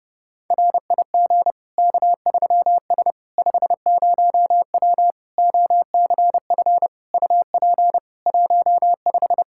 Morse code, heard as a keyed tone, about 30 wpm, 700 Hz. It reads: RIG K3S 50W OCF UP 15
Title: K1USN SST QSO Practice 30wpm